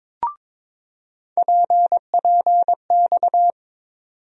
Morse code, heard as a keyed tone, about 22 wpm, 700 Hz. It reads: PPX
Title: USA Call Sign Postfixes - 3 Letters 22wpm